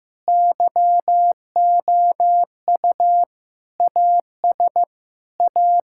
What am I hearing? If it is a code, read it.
Morse code, heard as a keyed tone, about 15 wpm, 700 Hz. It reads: YOU AS A
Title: Sentences from Top 1000 Words - Review Part 6 - 15wpm